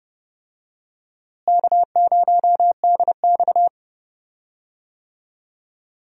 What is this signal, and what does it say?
Morse code, heard as a keyed tone, about 31 wpm, 700 Hz. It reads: K0DX
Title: USA Call Signs - 4 Characters - Repeated 2x - 30wpm